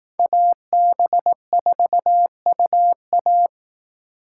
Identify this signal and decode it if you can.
Morse code, heard as a keyed tone, about 20 wpm, 700 Hz. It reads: AB4UA